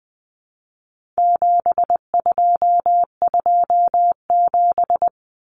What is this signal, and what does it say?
Morse code, heard as a keyed tone, about 20 wpm, 700 Hz. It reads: 7227
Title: Common Cut Numbers - 4-Digit 20wpm